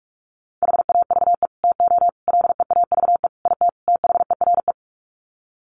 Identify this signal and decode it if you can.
Morse code, heard as a keyed tone, about 44 wpm, 700 Hz. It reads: HAVE TO LEAVE IT THERE
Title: Sets of 5 Words Encore2 45wpm